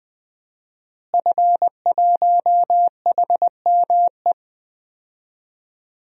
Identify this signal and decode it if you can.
Morse code, heard as a keyed tone, about 21 wpm, 700 Hz. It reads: F1HME
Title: DXCC-100 Call Signs - 5 Characters - Repeated 3x - 20wpm